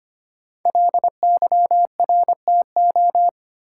Morse code, heard as a keyed tone, about 25 wpm, 700 Hz. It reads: LYRTO
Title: Code Groups - 5 Letters - 25wpm